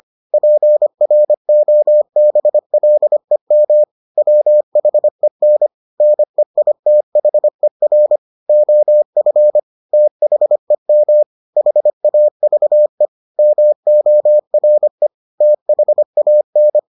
Morse code, heard as a keyed tone, about 25 wpm, 600 Hz. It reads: PROBLEM WHEN NEITHER OF THEM HAVE MORE THAN